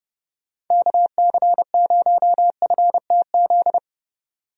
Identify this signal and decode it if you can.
Morse code, heard as a keyed tone, about 30 wpm, 700 Hz. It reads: KC0FTZ
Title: USA Call Signs - 6 Characters - 30wpm